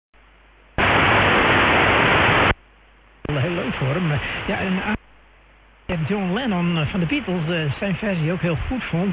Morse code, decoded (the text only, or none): none